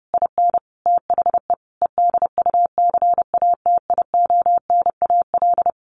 Morse code, heard as a keyed tone, about 30 wpm, 700 Hz. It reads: IN THE EDUCATIONAL
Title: Sentences from Top 1200 Words 30wpm